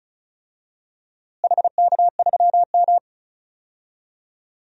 Morse code, heard as a keyed tone, about 36 wpm, 700 Hz. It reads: HK3M
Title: DXCC-100 Call Signs - 4 Characters - Repeated 3x - 35wpm